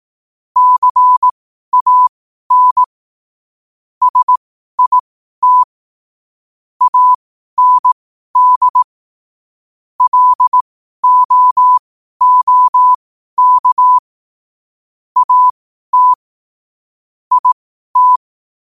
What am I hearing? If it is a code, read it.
Morse code, heard as a keyed tone, about 18 wpm, spaced as for 12 wpm, 1000 Hz. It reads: CAN SIT AND LOOK AT IT
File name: qotdfiles / 12wpm